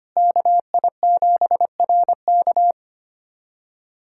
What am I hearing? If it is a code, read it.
Morse code, heard as a keyed tone, about 25 wpm, 700 Hz. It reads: KI7RK
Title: USA Callsign DE Callsign - 5 Characters - 25wpm